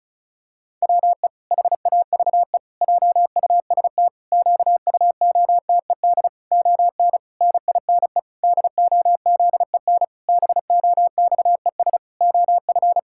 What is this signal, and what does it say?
Morse code, heard as a keyed tone, about 34 wpm, 700 Hz. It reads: WE HAVE JUST QUOTED ON NINE DOZEN BOXES OF
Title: Pangram Challenge v2 - 35wpm